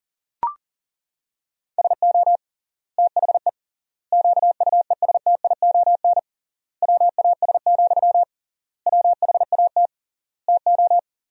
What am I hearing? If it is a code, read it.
Morse code, heard as a keyed tone, about 40 wpm, 700 Hz. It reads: SO THE QUESTION WAS, WHAT TO
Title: 3x Word-Spacing - Sentences from Top 400 Words - 40wpm